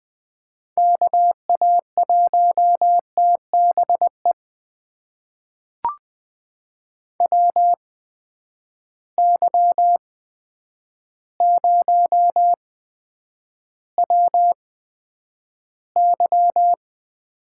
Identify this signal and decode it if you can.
Morse code, heard as a keyed tone, about 20 wpm, spaced as for 9 wpm, 700 Hz. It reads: KA1TBE WY0WY
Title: USA Call Signs - 6 Characters - 8x Character Spacing - 20wpm